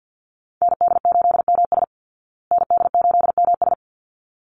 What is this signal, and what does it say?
Morse code, heard as a keyed tone, about 50 wpm, 700 Hz. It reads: ND8MS ND8MS
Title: Call Signs Repeated 3x - 50wpm